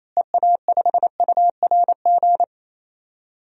Morse code, heard as a keyed tone, about 28 wpm, 700 Hz. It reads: EA5URG
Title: DXCC-100 Call Signs - 6 Characters - Repeated 3x - 28wpm